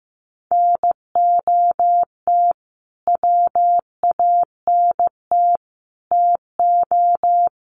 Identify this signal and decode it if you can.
Morse code, heard as a keyed tone, about 15 wpm, 700 Hz. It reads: NOT WANT TO
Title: Sets of 5 Words Encore 15wpm